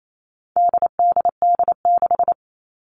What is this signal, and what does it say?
Morse code, heard as a keyed tone, about 28 wpm, 700 Hz. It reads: DDD6
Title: All Cut Numbers - 4-Digit 28wpm